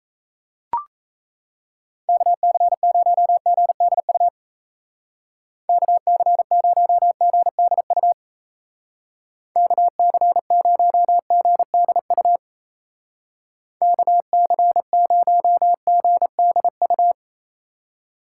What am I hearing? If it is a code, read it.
Morse code, heard as a keyed tone, about 34 wpm, 700 Hz. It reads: KC0GDU KC0GDU KC0GDU KC0GDU
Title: Speed Racing - USA 2x3 Call Signs 28wpm